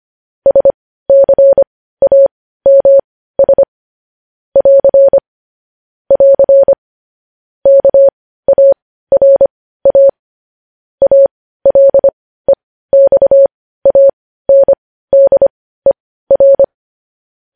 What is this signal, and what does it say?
Morse code, heard as a keyed tone, about 25 wpm, spaced as for 15 wpm, 550 Hz. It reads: SCAMS <AR> <AR> KARA ALEXANDER
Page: news-headlines-in-morse-code-at-15-wpm